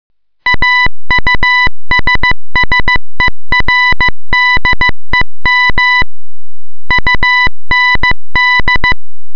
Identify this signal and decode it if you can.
Morse code, heard as a keyed tone, about 15 wpm, 2000 Hz. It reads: AUSSERDEM UND